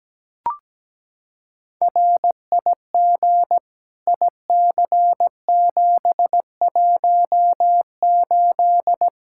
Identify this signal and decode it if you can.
Morse code, heard as a keyed tone, about 17 wpm, 700 Hz. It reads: RIG IC718
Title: Practice QSO - Exchange 2 - Info - 17wpm